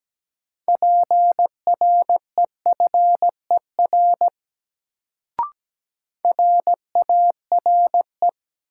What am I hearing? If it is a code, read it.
Morse code, heard as a keyed tone, about 17 wpm, 700 Hz. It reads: PREFER RARE
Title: Sentences from Top 1100 Words - New Word Drill Alt - 17wpm